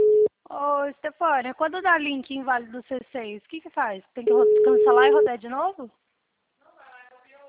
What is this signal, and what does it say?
Ringback tone (Europe, 425 Hz), with speech.
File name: out- -2012- - - .7547